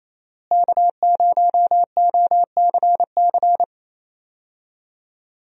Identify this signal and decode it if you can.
Morse code, heard as a keyed tone, about 28 wpm, 700 Hz. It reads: K0OCC
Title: USA Call Signs - 5 Characters - Repeated 2x - 28wpm